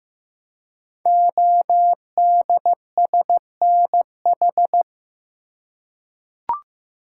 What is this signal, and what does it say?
Morse code, heard as a keyed tone, about 15 wpm, 700 Hz. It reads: ODSNH
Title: Code Groups - 5 Letter:Numbers - 15wpm